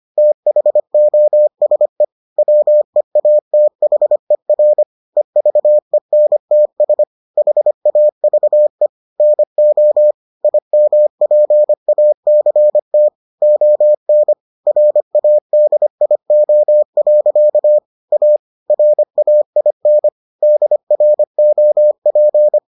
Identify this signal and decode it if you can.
Morse code, heard as a keyed tone, about 25 wpm, 600 Hz. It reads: THOSE WEATHER EVENTS HAVE NO IMPACT ON RADIO. A RAIN DROP